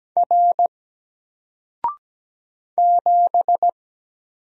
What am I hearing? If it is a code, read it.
Morse code, heard as a keyed tone, about 17 wpm, 700 Hz. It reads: R 7
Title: Single Letter-Number Alt 17wpm